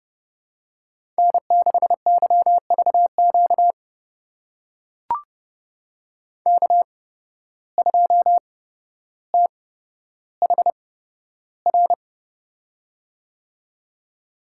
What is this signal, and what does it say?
Morse code, heard as a keyed tone, about 31 wpm, spaced as for 13 wpm, 700 Hz. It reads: N6YVQ K2THR
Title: USA Call Signs - 5 Characters - 8x Character Spacing - 30wpm